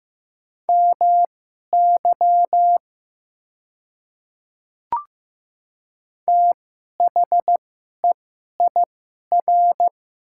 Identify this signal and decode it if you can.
Morse code, heard as a keyed tone, about 15 wpm, spaced as for 11 wpm, 700 Hz. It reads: MY THEIR
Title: 2x Character Spacing - Top 100 Words - 15wpm